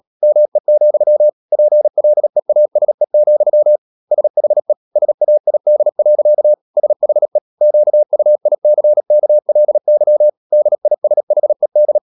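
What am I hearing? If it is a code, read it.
Morse code, heard as a keyed tone, about 37 wpm, 600 Hz. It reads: ME, PLEASE, SHE SAID. SHE QUICKLY DISHED
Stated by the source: Morse code practice at 35–39 wpm sending English text of Ken Follett's novel Never, from CW med Gnister.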